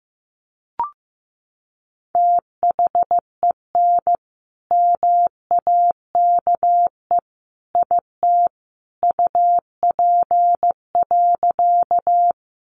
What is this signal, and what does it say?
Morse code, heard as a keyed tone, about 15 wpm, 700 Hz. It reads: THEN MAKE IT UP.
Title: Sentences from Top 100 Words 15wpm